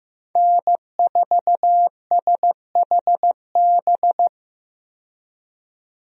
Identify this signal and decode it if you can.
Morse code, heard as a keyed tone, about 15 wpm, 700 Hz. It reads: N4SHB
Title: USA Call Signs - 5 Characters - 15wpm